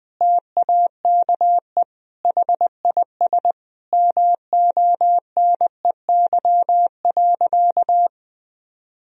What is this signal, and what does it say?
Morse code, heard as a keyed tone, about 20 wpm, 700 Hz. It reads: TAKE HIS MONEY.
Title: Sentences from Top 1000 Words - Review Part 4 - 20wpm